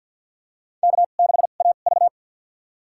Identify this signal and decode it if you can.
Morse code, heard as a keyed tone, about 51 wpm, spaced as for 36 wpm, 690 Hz. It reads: K6AV
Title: USA Call Signs - 2x Character Spacing - 50wpm